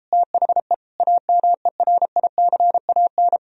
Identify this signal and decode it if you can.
Morse code, heard as a keyed tone, about 33 wpm, 700 Hz. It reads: THE AMERICAN